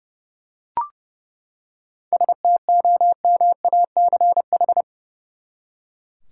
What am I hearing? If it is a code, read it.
Morse code, heard as a keyed tone, about 30 wpm, 700 Hz. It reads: STOMACH
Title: Seven Letter Words - 30wpm